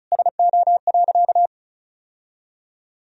Morse code, heard as a keyed tone, about 34 wpm, 700 Hz. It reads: SO.
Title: Sentences from Top 1000 Words - Review Part 1 - 35wpm